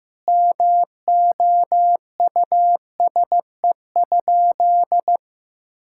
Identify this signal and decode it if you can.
Morse code, heard as a keyed tone, about 15 wpm, 700 Hz. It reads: MOUSE?
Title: Sentences from Top 1000 Words - Review Part 5 - 15wpm